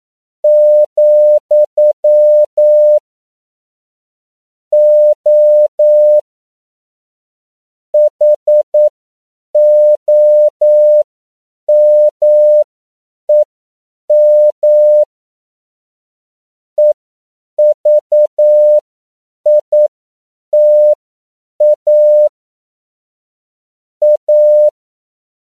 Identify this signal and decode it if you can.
Morse code, heard as a keyed tone, about 9 wpm, spaced as for 7 wpm, 600 Hz. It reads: , O HOMEM EVITA A
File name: CW / 10PPM